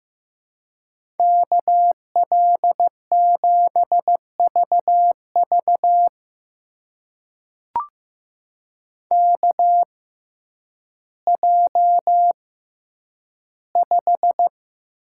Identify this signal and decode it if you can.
Morse code, heard as a keyed tone, about 15 wpm, spaced as for 9 wpm, 700 Hz. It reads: KL7VV KJ5
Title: USA Call Signs - 5 Characters - 6x Character Spacing - 15wpm